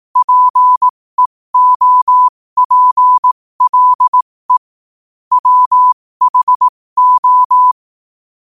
Morse code, heard as a keyed tone, about 18 wpm, spaced as for 15 wpm, 1000 Hz. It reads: PEOPLE WHO
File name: qotdfiles / 15wpm